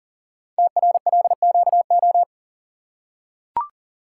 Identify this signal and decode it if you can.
Morse code, heard as a keyed tone, about 40 wpm, 690 Hz. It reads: TRLQO
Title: Code Groups - 5 Letter:Numbers - 40wpm